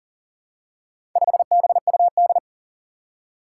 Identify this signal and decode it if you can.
Morse code, heard as a keyed tone, about 37 wpm, 700 Hz. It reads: 5BUD
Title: Four Challenging Letter-Number Combinations 40wpm